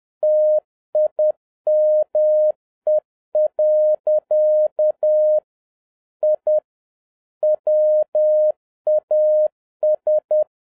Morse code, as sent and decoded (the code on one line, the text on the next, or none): - .. -- . .-.-.- / .. / .-- .- ...
TIME. I WAS